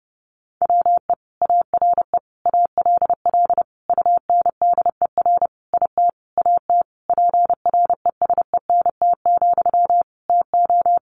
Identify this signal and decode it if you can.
Morse code, heard as a keyed tone, about 30 wpm, 700 Hz. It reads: WE ARE ALL UNDER IT AT PRESENT, TO
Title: Sentences from Top 1500 Words 30wpm